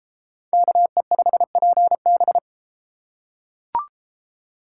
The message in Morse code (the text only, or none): KE5PB